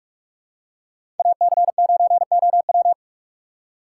Morse code, heard as a keyed tone, about 43 wpm, 700 Hz. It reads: AC9OW
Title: USA Call Signs - 5 Characters - 2x Character Spacing - 45wpm